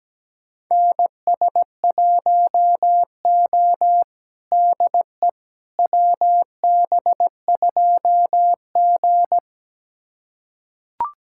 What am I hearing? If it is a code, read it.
Morse code, heard as a keyed tone, about 17 wpm, 700 Hz. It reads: NS1O DE WB2G
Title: USA Callsign DE Callsign - 4 Characters - 17wpm